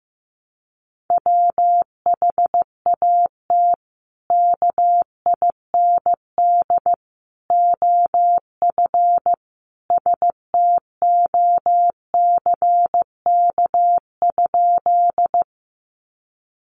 Morse code, heard as a keyed tone, about 15 wpm, 700 Hz. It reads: WHAT KIND OF STOCK?
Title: Sentences from Top 800 Words 15wpm